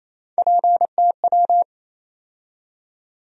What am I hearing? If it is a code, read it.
Morse code, heard as a keyed tone, about 28 wpm, 700 Hz. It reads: PTW
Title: USA Call Sign Postfixes - 3 Letters 28wpm